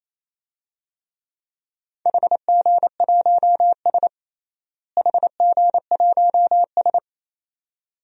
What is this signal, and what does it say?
Morse code, heard as a keyed tone, about 28 wpm, 700 Hz. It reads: HG1S HG1S
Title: DXCC-100 Call Signs - 4 Characters - Repeated 2x - 28wpm